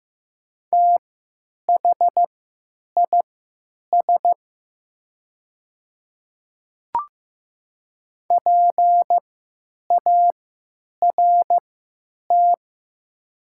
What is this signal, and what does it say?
Morse code, heard as a keyed tone, about 15 wpm, spaced as for 9 wpm, 700 Hz. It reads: THIS PART
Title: 3x Character Spacing - Top 200 Words - 15wpm